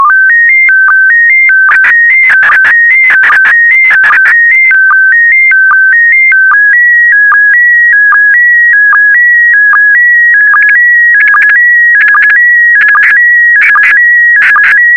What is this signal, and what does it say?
SSTV (slow-scan television), PD 160.